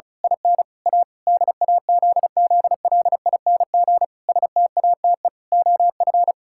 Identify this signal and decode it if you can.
Morse code, heard as a keyed tone, about 36 wpm, 700 Hz. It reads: IN A DAZZLING STATE OF
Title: Pangram Challenge v2 - 35wpm